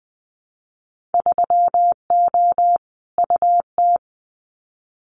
Morse code, heard as a keyed tone, about 20 wpm, 700 Hz. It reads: 3O UT
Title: Winter Field Day Exchanges 20wpm